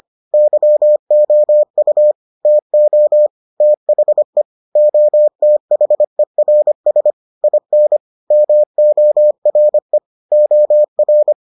Morse code, heard as a keyed tone, about 25 wpm, 600 Hz. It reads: YOU TO THE OTHERS IN MORE OR